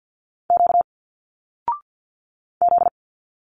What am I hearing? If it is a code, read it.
Morse code, heard as a keyed tone, about 50 wpm, 700 Hz. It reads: Q Z
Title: Single Letters 50wpm